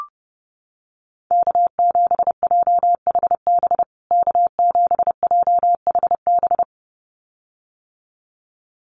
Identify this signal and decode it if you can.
Morse code, heard as a keyed tone, about 30 wpm, 700 Hz. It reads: K7JHB K7JHB
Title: USA Call Signs Repeated 2x 30wpm